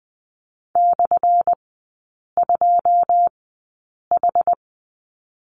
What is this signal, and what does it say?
Morse code, heard as a keyed tone, about 20 wpm, spaced as for 8 wpm, 690 Hz. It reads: /2H
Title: 191 Lesson 31e - Letter v - Three Characters 20wpm